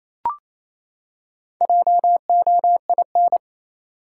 Morse code, heard as a keyed tone, about 27 wpm, 700 Hz. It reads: JOIN
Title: Four Letter Words - 28wpm